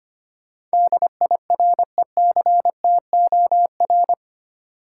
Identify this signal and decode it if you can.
Morse code, heard as a keyed tone, about 25 wpm, 700 Hz. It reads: DIRECTOR
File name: Eight Letter Words - 25wpm